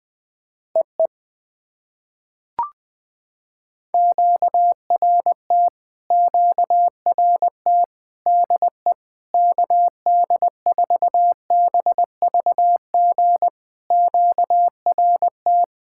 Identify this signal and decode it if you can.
Morse code, heard as a keyed tone, about 20 wpm, 700 Hz. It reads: EE QRT QRT DE KD4BVG QRT
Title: POTA QSO Practice - 20wpm